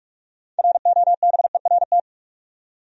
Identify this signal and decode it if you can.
Morse code, heard as a keyed tone, about 46 wpm, 700 Hz. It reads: ROBERT
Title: USA Common Names - 45wpm